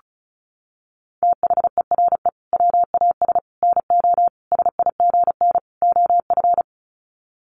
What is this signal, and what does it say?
Morse code, heard as a keyed tone, about 34 wpm, 700 Hz. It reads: THERE WAS NO SIGN OF
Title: Sets of 5 Words Encore2 35wpm